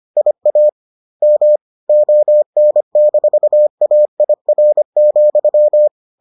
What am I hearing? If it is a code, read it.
Morse code, heard as a keyed tone, about 25 wpm, spaced as for 22 wpm, 600 Hz. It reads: IAM ON-AIR,